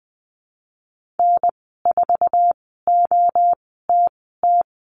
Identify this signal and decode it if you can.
Morse code, heard as a keyed tone, about 20 wpm, spaced as for 15 wpm, 700 Hz. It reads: N4OTT
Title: 082 Lesson 14g - Number 2 - Call Signs 20wpm